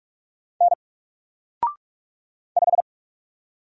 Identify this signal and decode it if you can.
Morse code, heard as a keyed tone, about 44 wpm, 700 Hz. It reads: N 5
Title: Single Letter-Number Alt 45wpm